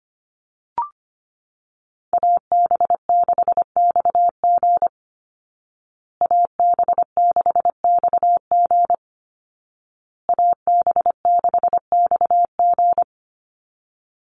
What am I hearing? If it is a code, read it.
Morse code, heard as a keyed tone, about 25 wpm, 700 Hz. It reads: AB6XG AB6XG AB6XG
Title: Call Signs Repeated 2x - 25wpm